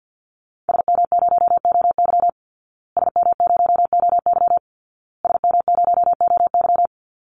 Morse code, heard as a keyed tone, about 50 wpm, 700 Hz. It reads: SM0OY SM0OY SM0OY
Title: Call Signs Repeated 2x - 50wpm